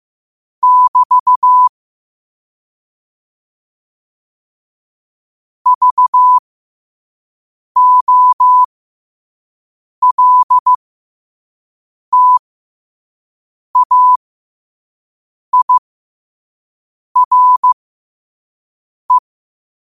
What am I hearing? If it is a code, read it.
Morse code, heard as a keyed tone, about 15 wpm, spaced as for 5 wpm, 1000 Hz. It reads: <BT> VOLTAIRE
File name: qotdfiles / 5wpm